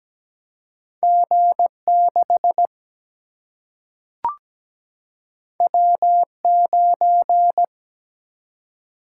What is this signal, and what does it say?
Morse code, heard as a keyed tone, about 17 wpm, 700 Hz. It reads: G 6 W 9
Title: Single-Letter Single-Number 17wpm